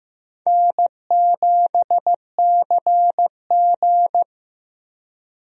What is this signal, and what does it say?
Morse code, heard as a keyed tone, about 15 wpm, 700 Hz. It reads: N7CG
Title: USA Call Signs - 4 Characters - 15wpm